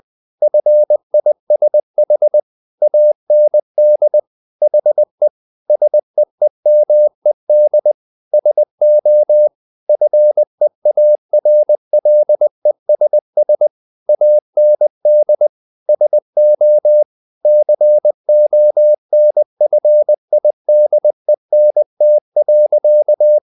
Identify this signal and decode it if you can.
Morse code, heard as a keyed tone, about 20 wpm, 600 Hz. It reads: FISH AND HE SEEMED SO FEARLESS AND SO CONFIDENT.
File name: the old man and the sea part 2 20wpm